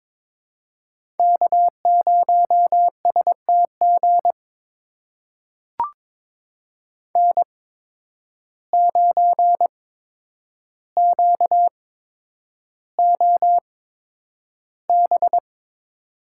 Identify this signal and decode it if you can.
Morse code, heard as a keyed tone, about 22 wpm, spaced as for 9 wpm, 700 Hz. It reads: K0STG N9QOB
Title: USA Call Signs - 5 Characters - 8x Character Spacing - 22wpm